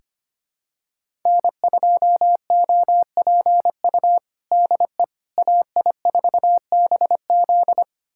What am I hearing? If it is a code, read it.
Morse code, heard as a keyed tone, about 25 wpm, 700 Hz. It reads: N2OPU DE AI4BZ
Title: USA Callsign DE Callsign 25wpm